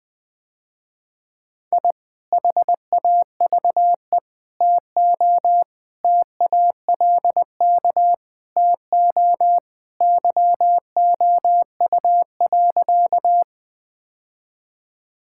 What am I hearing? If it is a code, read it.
Morse code, heard as a keyed tone, about 21 wpm, 700 Hz. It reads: I HAVE TO TALK TO YOU.
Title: Sentences with 3-Word Phrases - Spoken Beginning - 20wpm